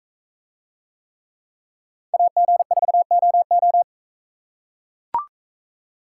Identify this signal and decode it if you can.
Morse code, heard as a keyed tone, about 43 wpm, 700 Hz. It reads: AG4OO